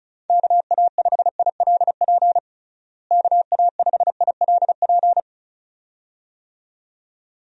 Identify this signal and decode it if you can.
Morse code, heard as a keyed tone, about 36 wpm, 700 Hz. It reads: KA5ILP KA5ILP
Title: USA Call Signs - 6 Characters - Repeated 2x - 35wpm